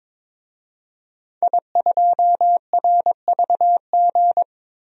Morse code, heard as a keyed tone, about 22 wpm, 700 Hz. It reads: I2RVG
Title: DXCC-100 Call Signs - 5 Characters - Repeated 3x - 22wpm